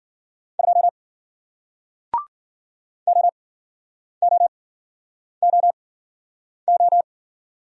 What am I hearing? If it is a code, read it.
Morse code, heard as a keyed tone, about 53 wpm, spaced as for 9 wpm, 700 Hz. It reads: 2 OOOO